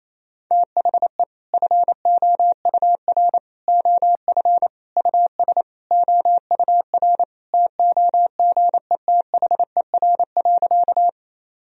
Morse code, heard as a keyed tone, about 28 wpm, 700 Hz. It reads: THE FOUR OF US OUR TOGETHER.
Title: Sentences from Top 1000 Words - Review Part 1 - 28wpm